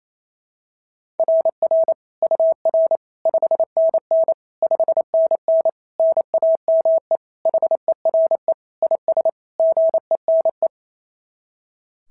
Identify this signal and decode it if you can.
Morse code, heard as a keyed tone, about 25 wpm, 650 Hz. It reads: RR UR 5NN 5NN NAME HERE IS GENE
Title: SOTA QSO Practice 28wpm